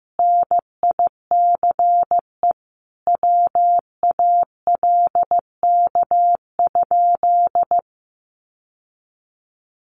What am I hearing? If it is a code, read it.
Morse code, heard as a keyed tone, about 14 wpm, 700 Hz. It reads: NICE WALK?
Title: Sentences from Top 1100 Words 15wpm